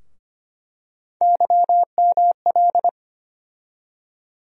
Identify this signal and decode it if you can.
Morse code, heard as a keyed tone, about 25 wpm, 700 Hz. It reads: YML
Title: USA Call Sign Postfixes - 3 Letters 25wpm